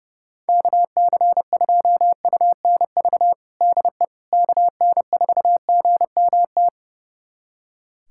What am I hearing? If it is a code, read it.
Morse code, heard as a keyed tone, about 30 wpm, 700 Hz. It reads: KC2UNV DE KN4GMT
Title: USA Callsign DE Callsign - 6 Characters - 30wpm